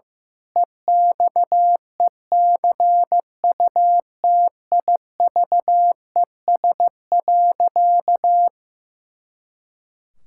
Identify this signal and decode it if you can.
Morse code, heard as a keyed tone, about 15 wpm, 700 Hz. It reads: EXECUTIVES.
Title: Pangram Challenge v2 - 15wpm